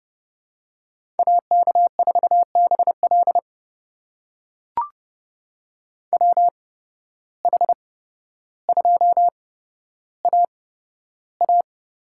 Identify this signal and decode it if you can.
Morse code, heard as a keyed tone, about 30 wpm, spaced as for 13 wpm, 700 Hz. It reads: AK4BL WH2AA
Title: USA Call Signs - 5 Characters - 8x Character Spacing - 30wpm